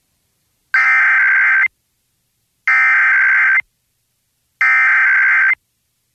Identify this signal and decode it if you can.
Emergency Alert System (SAME) tones (header bursts), heard as harsh buzzing bursts. It is heard from 0.9 to 5.5 s.